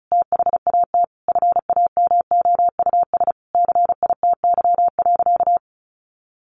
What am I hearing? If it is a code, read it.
Morse code, heard as a keyed tone, about 34 wpm, 700 Hz. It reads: THAT FAMOUS CITY.
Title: Sentences from Top 1500 Words 35wpm